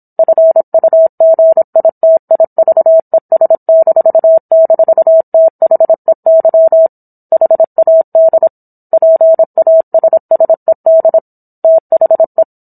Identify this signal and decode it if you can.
Morse code, heard as a keyed tone, about 27 wpm, 650 Hz. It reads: FUGITIVES--THEY HAD PASSED THE